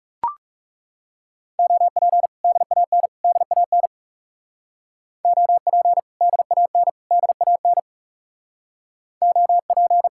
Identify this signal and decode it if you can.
Morse code, heard as a keyed tone, about 36 wpm, 700 Hz. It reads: OP DAN DAN OP DAN DAN OP
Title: Speed-Racing Common QSO Phrases - 30wpm